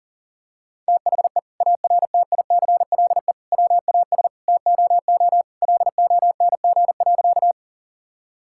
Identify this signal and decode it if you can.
Morse code, heard as a keyed tone, about 40 wpm, 700 Hz. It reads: THE ARTICLE WAS TOO LONG.
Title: Sentences from Top 1000 Words - Review Part 5 - 40wpm